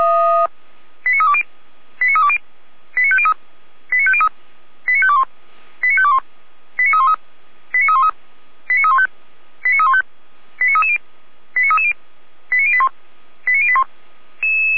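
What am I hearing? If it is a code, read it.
Five-tone selective call (ZVEI), heard as quick runs of five tones, tones of about 70 ms. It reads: 89320 89320 89503 89503 89521 89521 89313 89313 89315 89315 89330 89330 89081 89081
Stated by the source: ZVEI five-tone call received on SDR, from audio.f85360.feuerwehr-orlamuende.org (fire brigade Orlamünde).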